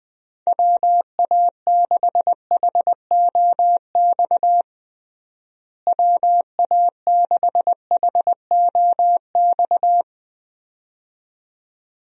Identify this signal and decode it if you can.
Morse code, heard as a keyed tone, about 20 wpm, 700 Hz. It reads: WA6HOX WA6HOX
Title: USA Call Signs - 6 Characters - Repeated 2x - 20wpm